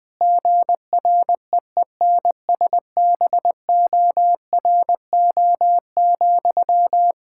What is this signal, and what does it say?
Morse code, heard as a keyed tone, about 21 wpm, 700 Hz. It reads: GREENSBORO,
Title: Practice QSO - Exchange 1 - Beginning - 20wpm